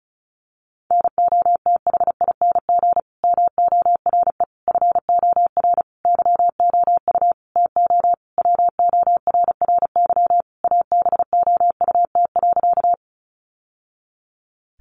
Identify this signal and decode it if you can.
Morse code, heard as a keyed tone, about 34 wpm, 700 Hz. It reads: NOTHING MORE FOR YOU TO WORRY ABOUT.
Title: Sentences from Top 1100 Words 35wpm